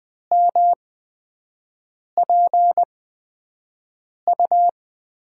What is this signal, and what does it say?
Morse code, heard as a keyed tone, about 20 wpm, spaced as for 5 wpm, 700 Hz. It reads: MPU
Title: USA Call Signs - 5 Characters - 8x Character Spacing - 20wpm